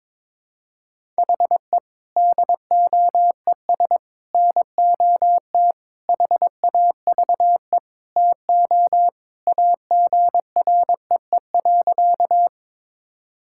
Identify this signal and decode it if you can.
Morse code, heard as a keyed tone, about 21 wpm, 700 Hz. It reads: HE DOES NOT HAVE TO AGREE.
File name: practice / 22wpm / Sentences with 4-Word Phrases - 22wpm